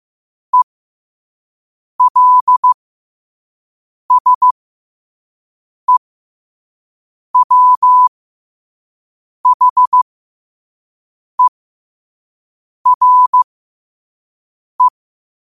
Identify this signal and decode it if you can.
Morse code, heard as a keyed tone, about 15 wpm, spaced as for 5 wpm, 1000 Hz. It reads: ELSEWHERE
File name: qotdfiles / 5wpm